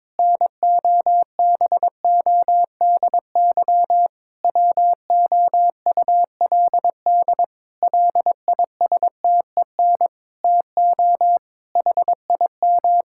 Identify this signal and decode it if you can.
Morse code, heard as a keyed tone, about 22 wpm, 700 Hz. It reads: NOBODY WOULD LISTEN TO HIM
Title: Sentences from Top 1600 Words 22wpm